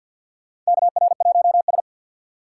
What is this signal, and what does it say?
Morse code, heard as a keyed tone, about 51 wpm, 700 Hz. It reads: KR1S
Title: USA Call Signs - 4 Characters - 3x Character Spacing - 50wpm